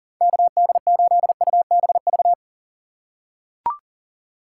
Morse code, heard as a keyed tone, about 39 wpm, 700 Hz. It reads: KD8UBV
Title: USA Call Signs - 2x Character Spacing - 40wpm